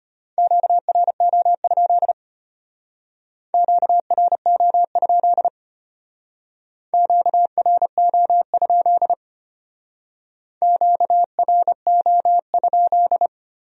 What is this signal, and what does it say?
Morse code, heard as a keyed tone, about 32 wpm, 700 Hz. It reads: QRO? QRO? QRO? QRO?